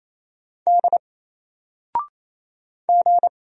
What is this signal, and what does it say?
Morse code, heard as a keyed tone, about 27 wpm, 700 Hz. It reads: D G
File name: Single-Letter-Number-28wpm